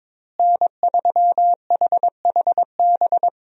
Morse code, heard as a keyed tone, about 22 wpm, 700 Hz. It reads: N3HHB
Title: USA Call Signs - 5 Characters - Repeated 2x - 22wpm